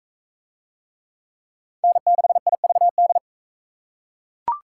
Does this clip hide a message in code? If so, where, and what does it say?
Yes: Morse code from 1.8 to 3.2 s, heard as a keyed tone, about 44 wpm, 700 Hz. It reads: N6IVD